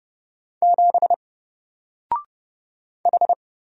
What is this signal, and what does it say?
Morse code, heard as a keyed tone, about 30 wpm, 700 Hz. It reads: 7 H
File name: Single-Letter-Number-30wpm